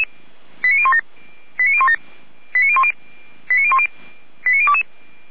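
Five-tone selective call (ZVEI), heard as quick runs of five tones, tones of about 70 ms. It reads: 89017 89017 89010 89022
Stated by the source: ZVEI five-tone selective call received on an SDR.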